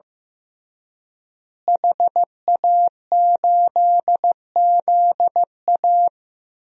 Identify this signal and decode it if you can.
Morse code, heard as a keyed tone, about 15 wpm, 700 Hz. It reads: HA8ZA
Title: DXCC-100 Call Signs - 5 Characters - Repeated 3x - 15wpm